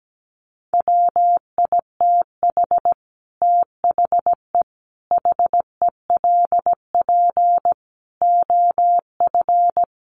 Morse code, heard as a keyed tone, about 17 wpm, 700 Hz. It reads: WITH THE HELP OF
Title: Sets of 5 Words 17wpm